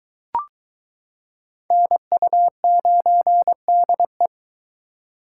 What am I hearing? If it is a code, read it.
Morse code, heard as a keyed tone, about 22 wpm, 700 Hz. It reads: NU9DE